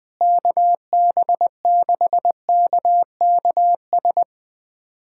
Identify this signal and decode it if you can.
Morse code, heard as a keyed tone, about 21 wpm, 700 Hz. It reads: KB6KKS
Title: USA Callsign DE Callsign - 6 Characters - 20wpm